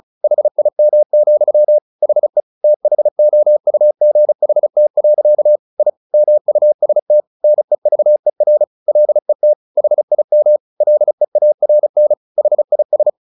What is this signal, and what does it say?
Morse code, heard as a keyed tone, about 36 wpm, 600 Hz. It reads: HIM, HE THOUGHT. I MUST NEVER LET HIM LEARN HIS
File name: the old man and the sea / the old man and the sea 1 35wpm